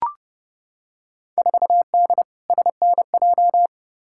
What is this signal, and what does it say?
Morse code, heard as a keyed tone, about 30 wpm, 700 Hz. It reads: 4D SNJ
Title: Field Day Exchanges Encore - 30wpm